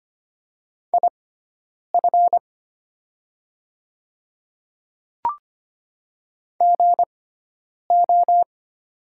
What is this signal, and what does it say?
Morse code, heard as a keyed tone, about 25 wpm, spaced as for 9 wpm, 700 Hz. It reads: IF GO
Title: 6x Character Spacing - Top 100 Words - 25wpm